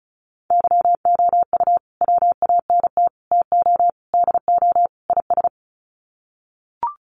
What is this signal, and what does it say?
Morse code, heard as a keyed tone, about 34 wpm, 700 Hz. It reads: YOU WANT TO DO IS
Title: Sets of 5 Words Encore2 35wpm